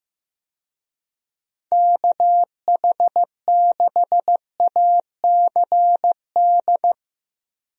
Morse code, heard as a keyed tone, about 15 wpm, 700 Hz. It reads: KH6ACD
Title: USA Call Signs - 6 Characters - Repeated 3x - 15wpm